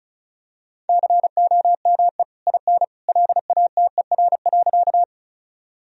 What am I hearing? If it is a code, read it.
Morse code, heard as a keyed tone, about 36 wpm, 700 Hz. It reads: COME IN LATER.
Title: Sentences from Top 1000 Words - Review Part 4 - 35wpm